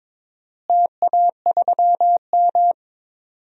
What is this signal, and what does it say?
Morse code, heard as a keyed tone, about 22 wpm, 700 Hz. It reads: TA3M
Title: DXCC-100 Call Signs - 4 Characters - Repeated 3x - 22wpm